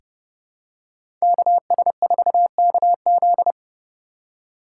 Morse code, heard as a keyed tone, about 30 wpm, 700 Hz. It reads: KS4KZ
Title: USA Call Signs - 5 Characters - 2x Character Spacing - 30wpm